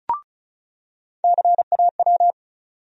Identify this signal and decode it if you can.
Morse code, heard as a keyed tone, about 36 wpm, 700 Hz. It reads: CAW
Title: USA Call Sign Postfixes - 3 Letters 35wpm